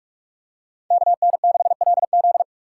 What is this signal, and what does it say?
Morse code, heard as a keyed tone, about 45 wpm, 690 Hz. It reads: KN6LZ
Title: USA Call Signs - 4x Character Spacing - 45wpm